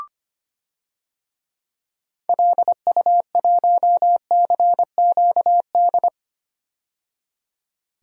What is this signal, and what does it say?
Morse code, heard as a keyed tone, about 25 wpm, 700 Hz. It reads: LU1CQD
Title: DXCC-100 Call Signs - 6 Characters - 25wpm